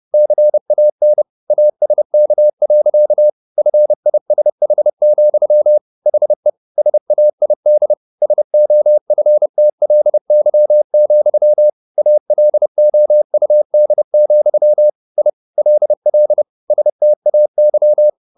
Morse code, heard as a keyed tone, about 30 wpm, 600 Hz. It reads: CAN ASK. FISH, HE SAID SOFTLY, ALOUD, I LL STAY
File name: the old man and the sea / the old man and the sea 1 30wpm